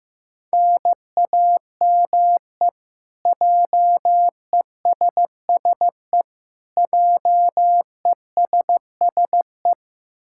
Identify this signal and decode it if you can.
Morse code, heard as a keyed tone, about 16 wpm, 700 Hz. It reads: NAME JESSE JESSE
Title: K1USN SST QSO Practice 15wpm